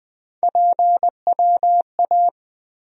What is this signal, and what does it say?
Morse code, heard as a keyed tone, about 20 wpm, 700 Hz. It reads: PWA
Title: USA Call Sign Postfixes - 3 Letters 20wpm